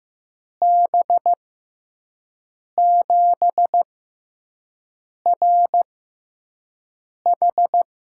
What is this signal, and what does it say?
Morse code, heard as a keyed tone, about 15 wpm, spaced as for 5 wpm, 700 Hz. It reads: B7RH
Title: USA Call Signs - 5 Characters - 6x Character Spacing - 15wpm